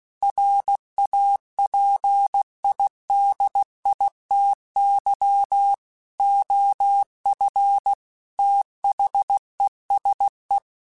Morse code, heard as a keyed tone, about 16 wpm, 800 Hz. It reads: RAPIDITY OF THESE
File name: Technical Subjects / Technical Subjects-mp3s / Reactance @15